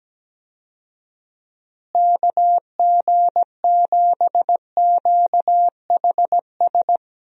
Morse code, heard as a keyed tone, about 17 wpm, 700 Hz. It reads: KG7QHS
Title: USA Call Signs - 6 Characters - Repeated 2x - 17wpm